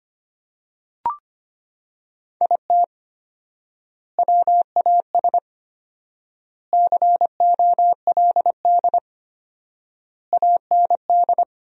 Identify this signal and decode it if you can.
Morse code, heard as a keyed tone, about 25 wpm, 700 Hz. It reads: IT WAS COLD AND
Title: 4x Word-Spacing - Sentences from Top 500 Words - 25wpm